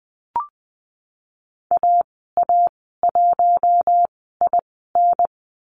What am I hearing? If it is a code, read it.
Morse code, heard as a keyed tone, about 20 wpm, 700 Hz. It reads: A A 1 I N
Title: 055 Lesson 10g - Letter r - Call Signs 20wpm